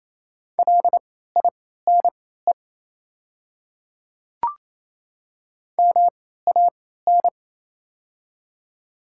Morse code, heard as a keyed tone, about 27 wpm, spaced as for 16 wpm, 690 Hz. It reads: LINE MAN
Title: 3x Character Spacing - Top 200 Words - 28wpm